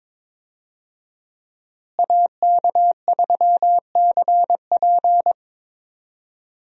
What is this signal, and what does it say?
Morse code, heard as a keyed tone, about 22 wpm, 700 Hz. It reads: AK3CP
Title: USA Call Signs - 5 Characters - Repeated 2x - 22wpm